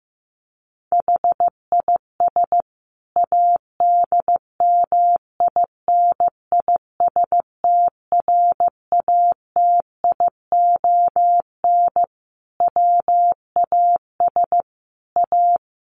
Morse code, heard as a keyed tone, about 15 wpm, 700 Hz. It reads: HIS ADMINISTRATION WAS A
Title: Sentences from Top 1300 Words 15wpm